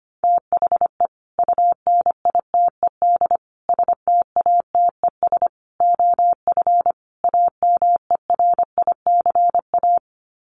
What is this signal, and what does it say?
Morse code, heard as a keyed tone, about 25 wpm, 700 Hz. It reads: THE UNITED STATES OF AMERICA
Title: Sets of 5 Words 25wpm